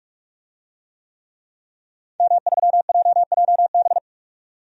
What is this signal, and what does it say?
Morse code, heard as a keyed tone, about 45 wpm, 700 Hz. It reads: M3JJB
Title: DXCC-100 Call Signs - 5 Characters - 45wpm